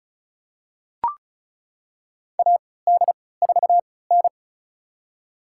Morse code, heard as a keyed tone, about 34 wpm, spaced as for 20 wpm, 700 Hz. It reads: AD4N
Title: USA Call Signs - 4 Characters - 3x Character Spacing - 35wpm